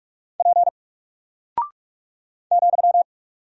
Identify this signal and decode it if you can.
Morse code, heard as a keyed tone, about 44 wpm, 700 Hz. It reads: P ,
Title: Single Character (Letter:Number:Punctuation:Prosign) Alt 45wpm